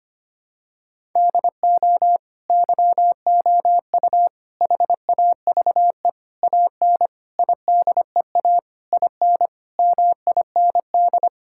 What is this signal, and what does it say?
Morse code, heard as a keyed tone, about 25 wpm, 700 Hz. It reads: DO YOU HAVE AN IDEA IN MIND
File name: practice / 25wpm / Sentences with 9-Word Phrases - 25wpm